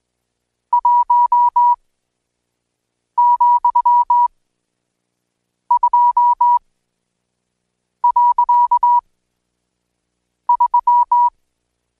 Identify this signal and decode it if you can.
Morse code, heard as a keyed tone, about 20 wpm, spaced as for 5 wpm, 970 Hz. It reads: 1,2.3
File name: Code Quick / AudioDownld1 / Seg76-Day8-DayDrill 8